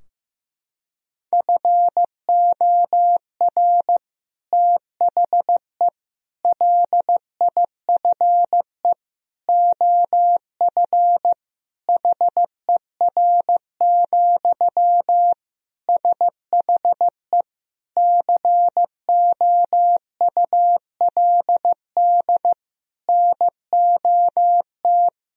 Morse code, heard as a keyed tone, about 15 wpm, 700 Hz. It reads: FOR THE LIFE OF HER, SHE COULD NOT
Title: Sentences from Top 1000 Words - Review Part 2 - 15wpm